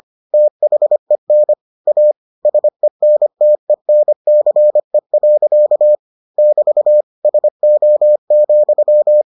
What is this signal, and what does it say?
Morse code, heard as a keyed tone, about 25 wpm, 600 Hz. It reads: THEN A SENTENCE. <BT> SO,